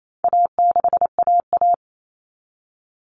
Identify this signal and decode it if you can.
Morse code, heard as a keyed tone, about 28 wpm, 700 Hz. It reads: A6AA
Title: All Cut Numbers - 4-Digit 28wpm